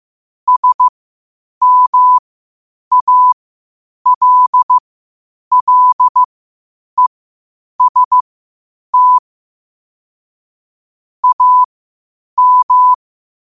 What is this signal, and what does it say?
Morse code, heard as a keyed tone, about 15 wpm, spaced as for 8 wpm, 1000 Hz. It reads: SMALLEST AM
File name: qotdfiles / 8wpm